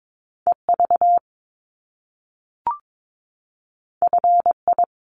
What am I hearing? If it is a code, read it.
Morse code, heard as a keyed tone, about 22 wpm, 700 Hz. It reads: EV FI
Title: Top Two Letter Combinations in Words 22wpm